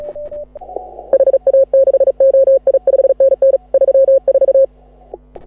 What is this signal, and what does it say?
Morse code, heard as a keyed tone, about 34 wpm, 530 Hz. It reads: HA6OI 5NN 34